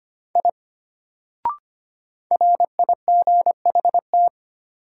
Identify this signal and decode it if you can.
Morse code, heard as a keyed tone, about 25 wpm, 700 Hz. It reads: I RIGHT
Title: Sentences from Top 200 Words - New Word Drill - 25wpm